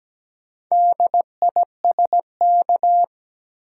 Morse code, heard as a keyed tone, about 17 wpm, 700 Hz. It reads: DISK
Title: Sentences from Top 1000 Words - Review Part 6 - New Word Drill - 17wpm